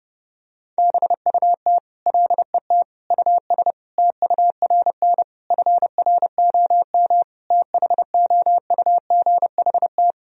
Morse code, heard as a keyed tone, about 30 wpm, 700 Hz. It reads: BUT LET US TURN FROM THOUGHT
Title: Sentences from Top 1000 Words - Review Part 4 - 30wpm